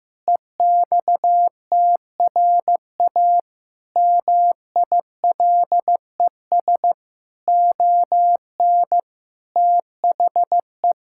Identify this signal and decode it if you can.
Morse code, heard as a keyed tone, about 14 wpm, 700 Hz. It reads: EXTRA MILES ON THE
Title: Pangram Challenge v2 - 15wpm